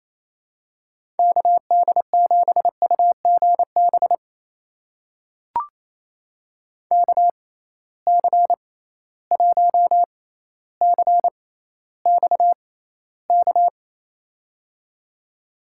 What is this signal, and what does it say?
Morse code, heard as a keyed tone, about 28 wpm, spaced as for 14 wpm, 700 Hz. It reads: KD7UGB KC1CXK
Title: USA Call Signs - 6 Characters - 6x Character Spacing - 28wpm